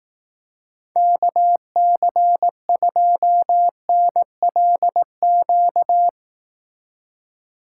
Morse code, heard as a keyed tone, about 18 wpm, 700 Hz. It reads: KC2NLQ
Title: Speed Racing - USA 2x3 Call Signs 15wpm